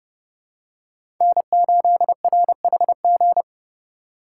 Morse code, heard as a keyed tone, about 32 wpm, 700 Hz. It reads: N8RHG
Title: USA Call Signs - 5 Characters - 2x Character Spacing - 30wpm